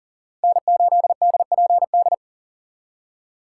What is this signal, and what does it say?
Morse code, heard as a keyed tone, about 40 wpm, 700 Hz. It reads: N8DPD
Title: K1USN SST QSO Practice - 40wpm